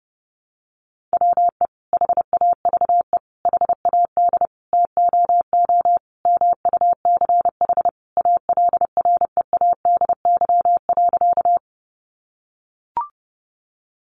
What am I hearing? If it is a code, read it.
Morse code, heard as a keyed tone, about 30 wpm, 700 Hz. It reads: WE HAVE HAD TOO MUCH ALREADY.
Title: Sentences from Top 700 Words 30wpm